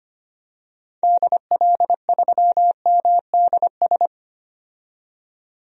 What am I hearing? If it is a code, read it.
Morse code, heard as a keyed tone, about 25 wpm, 700 Hz. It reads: DL3MDS
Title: DXCC-100 Call Signs - 6 Characters - 25wpm